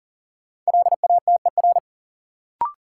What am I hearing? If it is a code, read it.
Morse code, heard as a keyed tone, about 40 wpm, 700 Hz. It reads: LATER
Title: Sentences from Top 700 Words - New Word Drill - 40wpm